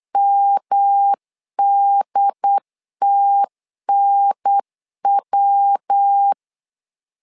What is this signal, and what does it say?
Morse code, heard as a keyed tone, about 8 wpm, 800 Hz. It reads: MDTNW
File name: LEZIONI / 4 Q1 8wpm 3035